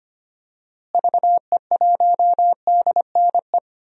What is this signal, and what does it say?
Morse code, heard as a keyed tone, about 25 wpm, 700 Hz. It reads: VE1DNE
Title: DXCC-100 Call Signs - 6 Characters - Repeated 2x - 25wpm